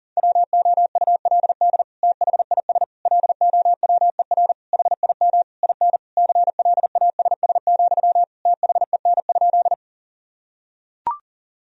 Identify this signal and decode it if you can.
Morse code, heard as a keyed tone, about 40 wpm, 700 Hz. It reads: WOULD THIS LOWER HIM IN CLASS, THEN?
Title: Sentences from Top 1000 Words - Review Part 4 - 40wpm